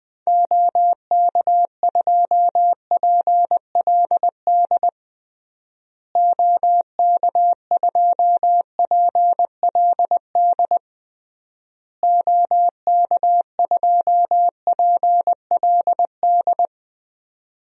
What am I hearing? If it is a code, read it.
Morse code, heard as a keyed tone, about 20 wpm, 700 Hz. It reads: OK2PLD OK2PLD OK2PLD
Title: DXCC-100 Call Signs - 6 Characters - Repeated 3x - 20wpm